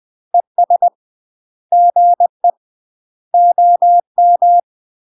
Morse code, heard as a keyed tone, about 20 wpm, 700 Hz. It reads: ES GE OM
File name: UK MASTER QSOs TRAINING/20WPM